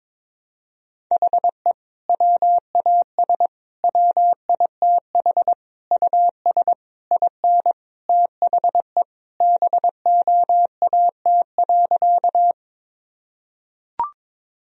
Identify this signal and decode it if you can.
Morse code, heard as a keyed tone, about 22 wpm, 700 Hz. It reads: HE WAS WITH US IN THE BOAT.
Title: Sentences from Top 1000 Words - Review Part 3 - 22wpm